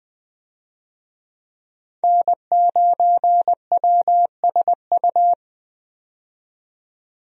Morse code, heard as a keyed tone, about 20 wpm, 700 Hz. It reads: N9WSU
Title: USA Call Signs - 5 Characters - 20wpm